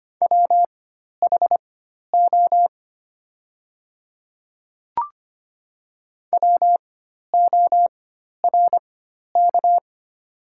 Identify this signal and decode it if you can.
Morse code, heard as a keyed tone, about 25 wpm, spaced as for 12 wpm, 690 Hz. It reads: WHO WORK
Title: 4x Character Spacing - Top 500 Words - 25wpm